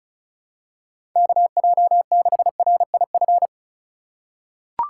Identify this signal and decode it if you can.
Morse code, heard as a keyed tone, about 36 wpm, 700 Hz. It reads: KJ6RIF
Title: USA Call Signs - 6 Characters - 2x Character Spacing - 35wpm